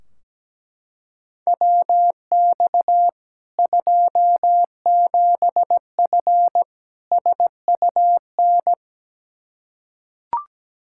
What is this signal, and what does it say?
Morse code, heard as a keyed tone, about 17 wpm, 700 Hz. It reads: WX 27F SUN
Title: Practice QSO - Exchange 2 - Info - 17wpm